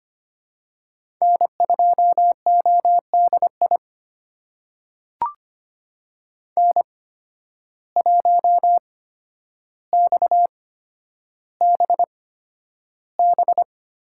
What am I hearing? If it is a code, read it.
Morse code, heard as a keyed tone, about 25 wpm, spaced as for 11 wpm, 700 Hz. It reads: N2ODI N1XBB
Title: USA Call Signs - 5 Characters - 8x Character Spacing - 25wpm